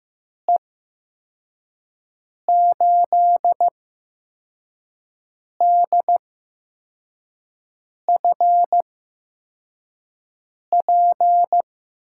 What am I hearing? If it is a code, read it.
Morse code, heard as a keyed tone, about 15 wpm, spaced as for 4 wpm, 700 Hz. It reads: E8DFP
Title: USA Call Signs - 6 Characters - 8x Character Spacing - 15wpm